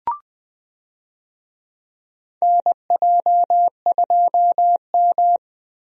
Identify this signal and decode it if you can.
Morse code, heard as a keyed tone, about 21 wpm, 700 Hz. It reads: NJ2M
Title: USA Call Signs - 4 Characters - Repeated 2x - 20wpm